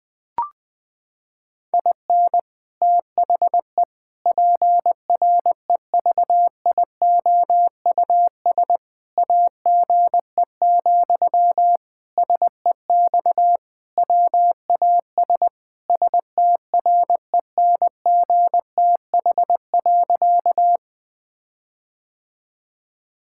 Morse code, heard as a keyed tone, about 21 wpm, 700 Hz. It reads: IN THE PREVIOUS AGE, SEX WAS STRENGTH.
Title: Sentences from Top 1600 Words 20wpm